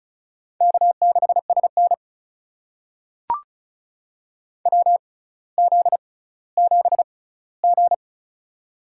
Morse code, heard as a keyed tone, about 36 wpm, spaced as for 18 wpm, 700 Hz. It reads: K6SN WZ7G
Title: USA Call Signs - 4 Characters - 6x Character Spacing - 35wpm